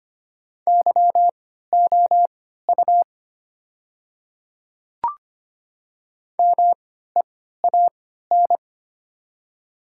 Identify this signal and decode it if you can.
Morse code, heard as a keyed tone, about 25 wpm, spaced as for 14 wpm, 700 Hz. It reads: YOU MEAN
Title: 3x Character Spacing - Top 200 Words - 25wpm